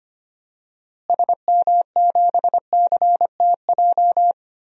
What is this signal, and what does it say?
Morse code, heard as a keyed tone, about 25 wpm, 700 Hz. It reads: SM7CTJ
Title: DXCC-100 Call Signs - 6 Characters - Repeated 2x - 25wpm